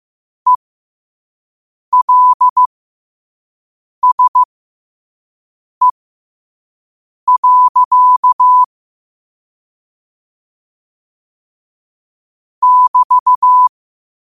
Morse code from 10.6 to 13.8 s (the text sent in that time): <BT>